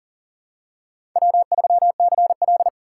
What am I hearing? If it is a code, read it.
Morse code, heard as a keyed tone, about 40 wpm, 700 Hz. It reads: W3CL
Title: USA Call Signs - 4 Characters - 4x Character Spacing - 40wpm